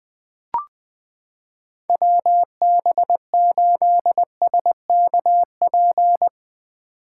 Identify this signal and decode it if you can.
Morse code, heard as a keyed tone, about 20 wpm, 700 Hz. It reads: WB8SKP
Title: K1USN SST QSO Practice - 20wpm